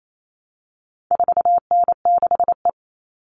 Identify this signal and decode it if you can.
Morse code, heard as a keyed tone, about 28 wpm, 700 Hz. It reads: 4N6E
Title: All Cut Numbers - 4-Digit 28wpm